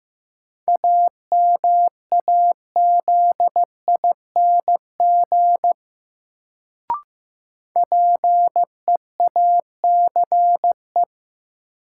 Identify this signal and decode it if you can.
Morse code, heard as a keyed tone, about 15 wpm, 700 Hz. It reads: AMAZING PEACE
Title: Sentences from Top 1500 Words - New Word Drill - 15wpm